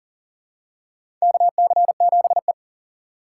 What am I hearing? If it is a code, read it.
Morse code, heard as a keyed tone, about 40 wpm, 700 Hz. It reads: KC7E
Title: USA Call Signs - 4 Characters - 4x Character Spacing - 40wpm